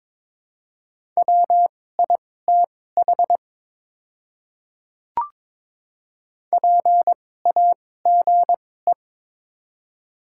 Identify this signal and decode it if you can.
Morse code, heard as a keyed tone, about 23 wpm, spaced as for 16 wpm, 700 Hz. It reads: WITH PAGE
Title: 2x Character Spacing - Top 300 Words - 22wpm